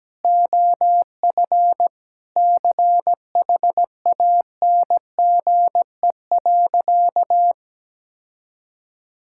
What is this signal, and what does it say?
Morse code, heard as a keyed tone, about 17 wpm, 700 Hz. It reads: OF CHANGE.
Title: Sentences from Top 1000 Words - Review Part 5 - 17wpm